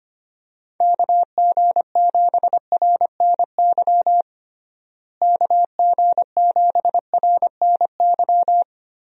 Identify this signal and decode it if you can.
Morse code, heard as a keyed tone, about 25 wpm, 700 Hz. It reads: KG7RNY KG7RNY
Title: USA Call Signs - 6 Characters - Repeated 3x - 25wpm